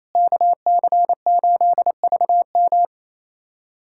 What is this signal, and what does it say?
Morse code, heard as a keyed tone, about 27 wpm, 700 Hz. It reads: KC8VM
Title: USA Callsign DE Callsign - 5 Characters - 28wpm